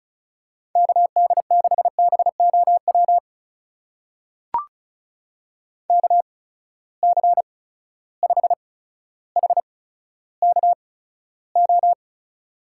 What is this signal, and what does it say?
Morse code, heard as a keyed tone, about 36 wpm, spaced as for 15 wpm, 700 Hz. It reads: KD6BOW KC5HKO
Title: USA Call Signs - 6 Characters - 8x Character Spacing - 35wpm